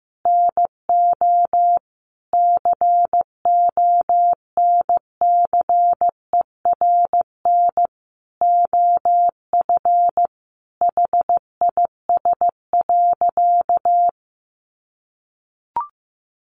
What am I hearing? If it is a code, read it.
Morse code, heard as a keyed tone, about 15 wpm, 700 Hz. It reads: NO CONCERN OF HIS.
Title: Sentences from Top 1000 Words 15wpm